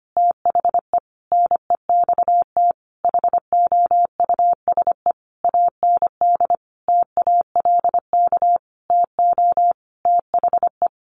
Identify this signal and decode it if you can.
Morse code, heard as a keyed tone, about 25 wpm, 700 Hz. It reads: THE NEXT HOUSE AND TALK TO THE
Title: Sentences from Top 400 Words 25wpm